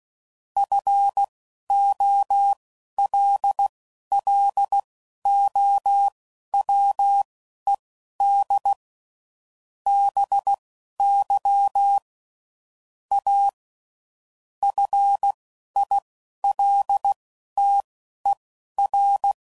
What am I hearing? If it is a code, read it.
Morse code, heard as a keyed tone, about 16 wpm, spaced as for 12 wpm, 800 Hz. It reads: FOLLOWED BY A FILTER
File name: CW Code Groups @10WPM / Technical Subjects / SwPwrSup